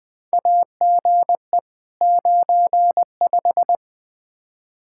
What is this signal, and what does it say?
Morse code, heard as a keyed tone, about 20 wpm, 700 Hz. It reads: AGE 95
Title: K1USN SST QSO Practice 20wpm